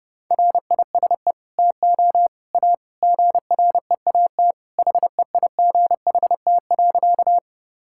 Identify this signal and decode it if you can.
Morse code, heard as a keyed tone, about 30 wpm, 700 Hz. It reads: RISE TO A GREAT HEIGHT.
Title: Sentences from Top 1600 Words 30wpm